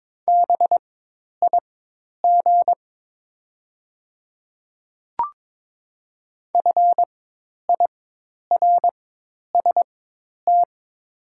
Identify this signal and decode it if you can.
Morse code, heard as a keyed tone, about 22 wpm, spaced as for 10 wpm, 700 Hz. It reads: BIG FIRST
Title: 4x Character Spacing - Top 200 Words - 22wpm